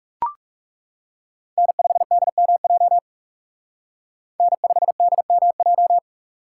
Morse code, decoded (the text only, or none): N5DMJ N5DMJ